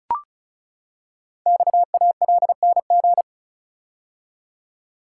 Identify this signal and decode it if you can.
Morse code, heard as a keyed tone, about 36 wpm, 700 Hz. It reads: XALNG
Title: Code Groups - 5 Letters - 35wpm